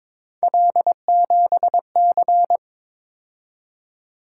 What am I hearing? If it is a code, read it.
Morse code, heard as a keyed tone, about 22 wpm, 700 Hz. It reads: L7C
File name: practice / 22wpm / Three Letter-Number Combinations - 22wpm